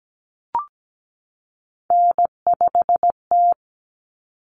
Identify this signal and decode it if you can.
Morse code, heard as a keyed tone, about 17 wpm, 700 Hz. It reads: N5T
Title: Common Cut Numbers - 3-Digit 17wpm